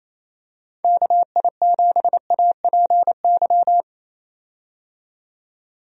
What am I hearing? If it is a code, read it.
Morse code, heard as a keyed tone, about 28 wpm, 700 Hz. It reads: KI7APY
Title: USA Call Signs - 6 Characters - 28wpm